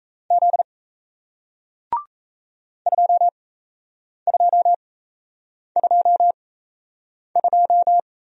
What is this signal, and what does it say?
Morse code, heard as a keyed tone, about 38 wpm, spaced as for 8 wpm, 700 Hz. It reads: Z 2222